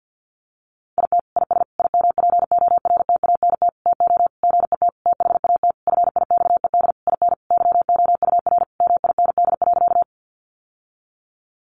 Morse code, heard as a keyed tone, about 50 wpm, 700 Hz. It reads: IT IS IMPORTANT TO GET THAT FIXED IN YOUR MIND.
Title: Sentences from Top 1100 Words 50wpm